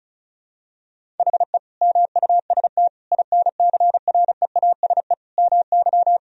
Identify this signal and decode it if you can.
Morse code, heard as a keyed tone, about 36 wpm, 700 Hz. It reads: HE MUST INCREASE MY
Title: Sentences from Top 1000 Words - Review Part 4 - 35wpm